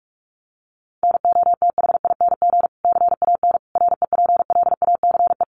Morse code, heard as a keyed tone, about 44 wpm, 700 Hz. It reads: NOTHING CAN REPLACE
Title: Sentences from Top 1400 Words 45wpm